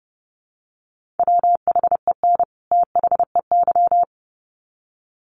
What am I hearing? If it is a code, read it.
Morse code, heard as a keyed tone, about 30 wpm, 700 Hz. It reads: WHEN THEY
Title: Sets of 2 Words 30wpm